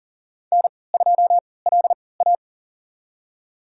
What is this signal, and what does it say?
Morse code, heard as a keyed tone, about 38 wpm, spaced as for 22 wpm, 700 Hz. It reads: N2LA
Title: USA Call Signs - 4 Characters - 3x Character Spacing - 40wpm